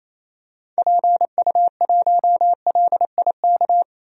Morse code, heard as a keyed tone, about 27 wpm, 700 Hz. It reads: PU1LIK
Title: DXCC-100 Call Signs - 6 Characters - 28wpm